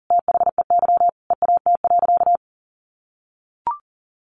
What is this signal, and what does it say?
Morse code, heard as a keyed tone, about 40 wpm, 700 Hz. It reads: THEY EAT.
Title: Sentences from Top 700 Words 40wpm